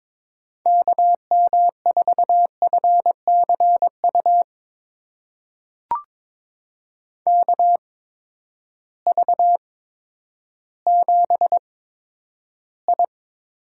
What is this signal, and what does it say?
Morse code, heard as a keyed tone, about 22 wpm, spaced as for 11 wpm, 700 Hz. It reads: KM4FCU KV7I
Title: USA Call Signs - 8x Character Spacing - 22wpm